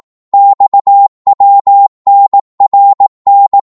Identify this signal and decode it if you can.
Morse code, heard as a keyed tone, about 18 wpm, 800 Hz. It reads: XWNRN
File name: Morse / ltr-18wpm